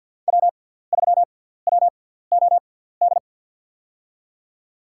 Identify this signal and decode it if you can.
Morse code, heard as a keyed tone, about 46 wpm, spaced as for 17 wpm, 700 Hz. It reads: W3WOD
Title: USA Call Signs - 5 Characters - 6x Character Spacing - 50wpm